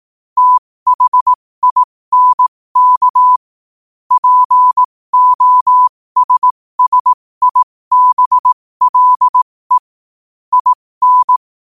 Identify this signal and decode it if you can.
Morse code, heard as a keyed tone, about 18 wpm, spaced as for 15 wpm, 1000 Hz. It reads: THINK POSSIBLE IN
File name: qotdfiles / 15wpm